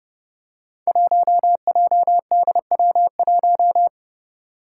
Morse code, heard as a keyed tone, about 30 wpm, 700 Hz. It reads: 1JDW1
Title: Five Challenging Letter-Number Combinations 30wpm